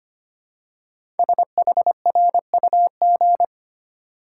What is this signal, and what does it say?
Morse code, heard as a keyed tone, about 25 wpm, 700 Hz. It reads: SHRUG
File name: 5 Letter Words - First 3 Letters Given - 25wpm